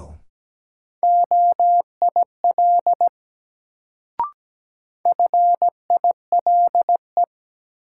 Morse code, heard as a keyed tone, about 17 wpm, 700 Hz. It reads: OIL FILE
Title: Sentences from Top 1000 Words - Review Part 4 - New Word Drill - 17wpm